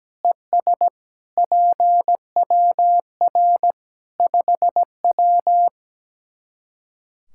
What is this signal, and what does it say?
Morse code, heard as a keyed tone, about 17 wpm, 700 Hz. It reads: ES PWR 5W
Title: K1USN SST QSO Practice 17wpm